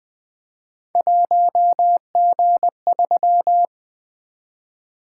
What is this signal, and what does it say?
Morse code, heard as a keyed tone, about 20 wpm, 700 Hz. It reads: 1G3
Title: Three Letter-Number Combinations - 20wpm